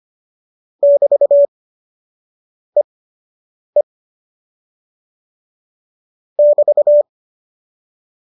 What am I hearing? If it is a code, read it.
Morse code, heard as a keyed tone, about 25 wpm, spaced as for 7 wpm, 570 Hz. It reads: <BT>EE <BT>